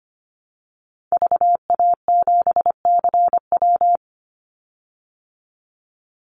Morse code, heard as a keyed tone, about 25 wpm, 700 Hz. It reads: VA7CW
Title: Call Signs Repeated 3x - 25wpm